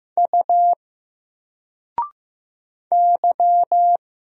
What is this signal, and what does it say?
Morse code, heard as a keyed tone, about 15 wpm, 700 Hz. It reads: U Y
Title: Single Letter-Number 15wpm